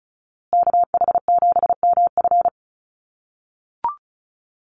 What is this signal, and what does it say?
Morse code, heard as a keyed tone, about 34 wpm, 700 Hz. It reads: KH7MF
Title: USA Call Signs Repeated 2x 35wpm